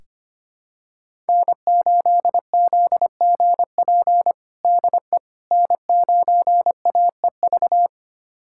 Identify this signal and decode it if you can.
Morse code, heard as a keyed tone, about 25 wpm, 700 Hz. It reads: N8ZGP DE N9AEV
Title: USA Callsign DE Callsign - 5 Characters - 25wpm